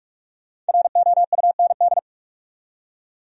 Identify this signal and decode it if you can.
Morse code, heard as a keyed tone, about 59 wpm, spaced as for 52 wpm, 700 Hz. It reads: ROUND
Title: Speed-Racing Top 300 Words 45wpm